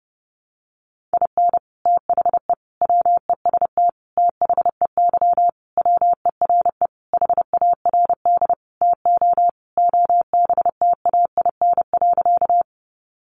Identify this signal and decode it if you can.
Morse code, heard as a keyed tone, about 28 wpm, 700 Hz. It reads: IN THE WEST THEY WERE HARD TO OBTAIN.
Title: Sentences from Top 1300 Words 30wpm